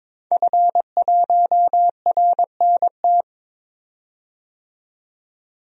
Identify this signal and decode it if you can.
Morse code, heard as a keyed tone, about 22 wpm, 700 Hz. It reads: F1RNT
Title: DXCC-100 Call Signs - 5 Characters - 22wpm